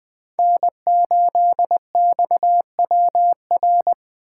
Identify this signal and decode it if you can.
Morse code, heard as a keyed tone, about 20 wpm, 700 Hz. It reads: N8XWR
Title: USA Call Signs - 2x Character Spacing - 20wpm